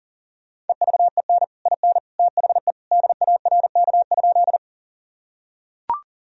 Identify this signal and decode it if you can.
Morse code, heard as a keyed tone, about 40 wpm, 700 Hz. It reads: EVEN IN THE DARK?
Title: Sentences from Top 1000 Words - Review Part 1 - 40wpm